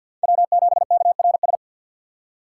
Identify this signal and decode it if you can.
Morse code, heard as a keyed tone, about 52 wpm, 700 Hz. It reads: W7KRS
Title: K1USN SST QSO Practice - 50wpm